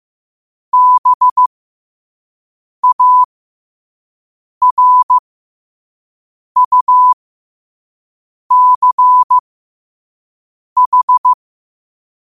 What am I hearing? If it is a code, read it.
Morse code, heard as a keyed tone, about 15 wpm, spaced as for 5 wpm, 1000 Hz. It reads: BARUCH
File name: qotdfiles / 5wpm